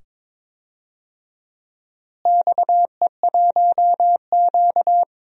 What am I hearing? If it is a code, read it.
Morse code, heard as a keyed tone, about 21 wpm, 700 Hz. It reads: XE1Q
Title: DXCC-100 Call Signs - 4 Characters - Repeated 2x - 22wpm